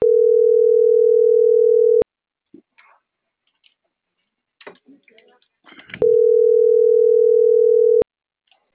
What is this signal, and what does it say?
Ringback tone (North America), 440 + 480 Hz.